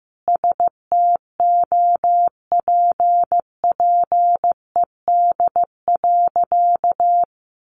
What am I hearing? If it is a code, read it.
Morse code, heard as a keyed tone, about 15 wpm, 700 Hz. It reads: STOPPED.
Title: A Sound Of Thunder 15wpm